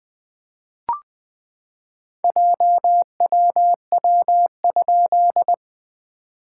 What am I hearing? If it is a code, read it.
Morse code, heard as a keyed tone, about 20 wpm, 700 Hz. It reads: JWW?
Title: SOTA QSO Practice 20wpm